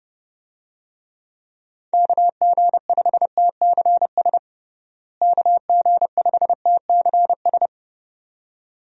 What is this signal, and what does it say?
Morse code, heard as a keyed tone, about 30 wpm, 700 Hz. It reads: KG5TCS KG5TCS
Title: USA Call Signs - 6 Characters - Repeated 2x - 30wpm